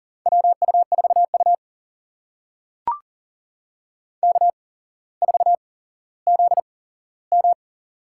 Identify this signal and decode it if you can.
Morse code, heard as a keyed tone, about 40 wpm, spaced as for 17 wpm, 700 Hz. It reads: WU4U K4ZM
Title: USA Call Signs - 8x Character Spacing - 40wpm